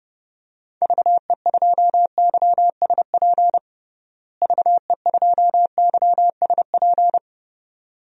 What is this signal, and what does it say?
Morse code, heard as a keyed tone, about 30 wpm, 700 Hz. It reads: VE2YSP VE2YSP
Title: DXCC-100 Call Signs - 6 Characters - Repeated 2x - 30wpm